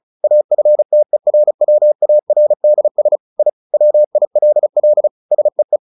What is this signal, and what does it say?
Morse code, heard as a keyed tone, about 35 wpm, 600 Hz. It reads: AFTERWARDS I WILL SEE
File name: the old man and the sea 2 35wpm